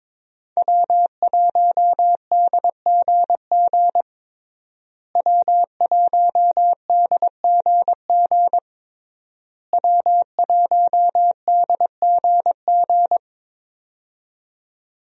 Morse code, heard as a keyed tone, about 22 wpm, 700 Hz. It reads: W1DGG W1DGG W1DGG
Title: USA Call Signs - 5 Characters - Repeated 3x - 22wpm